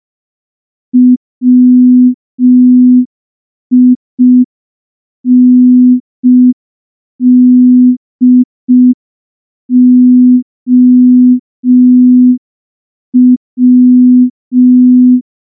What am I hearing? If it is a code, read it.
Morse code, heard as a keyed tone, about 5 wpm, 250 Hz. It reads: WINDOW
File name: morse / morse-training-05WPM